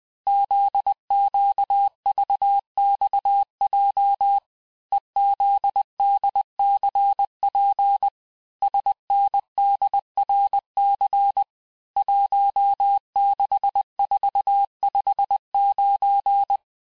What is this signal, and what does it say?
Morse code, heard as a keyed tone, about 20 wpm, 800 Hz. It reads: ZQVXJ EZDCP SNDRC 16459
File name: cours-858-texte-chiffre-vitesse-19mpm